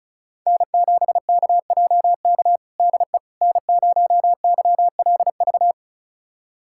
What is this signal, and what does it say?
Morse code, heard as a keyed tone, about 36 wpm, 690 Hz. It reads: N7KJK DE N0YLV
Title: USA Callsign DE Callsign - 5 Characters - 35wpm